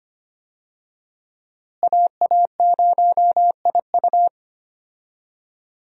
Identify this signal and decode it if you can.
Morse code, heard as a keyed tone, about 25 wpm, 700 Hz. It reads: AA0IU
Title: USA Call Signs - 5 Characters - Repeated 2x - 25wpm